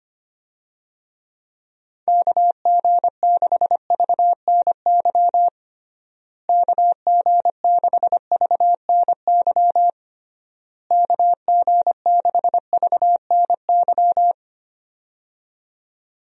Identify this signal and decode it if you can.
Morse code, heard as a keyed tone, about 25 wpm, 700 Hz. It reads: KG6VNY KG6VNY KG6VNY
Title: USA Call Signs - 6 Characters - Repeated 3x - 25wpm